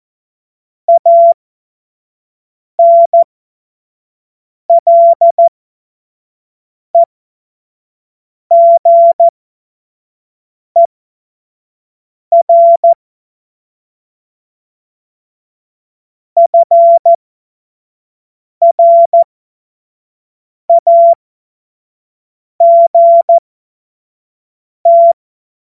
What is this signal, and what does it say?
Morse code, heard as a keyed tone, about 14 wpm, spaced as for 5 wpm, 680 Hz. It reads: ANLEGER FRAGT
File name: Morsekurs / Witze / Witz 27. Juni 2023 25 F70